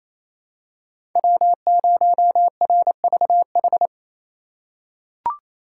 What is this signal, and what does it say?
Morse code, heard as a keyed tone, about 28 wpm, 700 Hz. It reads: W0RVH
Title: USA Call Signs - 5 Characters - 2x Character Spacing - 28wpm